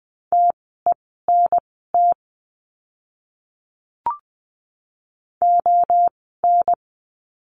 Morse code, heard as a keyed tone, about 20 wpm, spaced as for 14 wpm, 700 Hz. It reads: TENT ON
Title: 030 Lesson 6f - Letter i - Words 20wpm